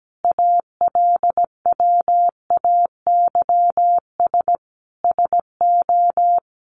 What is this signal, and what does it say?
Morse code, heard as a keyed tone, about 17 wpm, 700 Hz. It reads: ALWAYS SO
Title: Sentences from Top 700 Words 17wpm